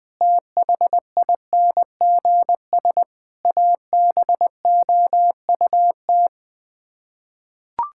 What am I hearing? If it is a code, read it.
Morse code, heard as a keyed tone, about 19 wpm, 700 Hz. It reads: THINGS ABOUT
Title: Sets of 6 Words - 20wpm